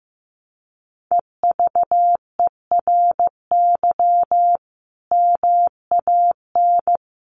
Morse code, heard as a keyed tone, about 15 wpm, 700 Hz. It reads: EVERY MAN
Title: Sentences from Top 600 Words 15wpm